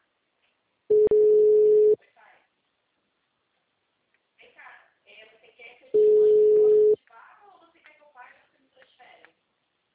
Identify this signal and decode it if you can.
Ringback tone (Europe, 425 Hz), with speech.
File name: out- -2019- - - .3198